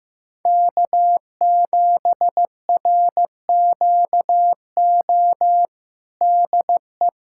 Morse code, heard as a keyed tone, about 15 wpm, 700 Hz. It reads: K7RQO DE
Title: USA Callsign DE Callsign - 5 Characters - 15wpm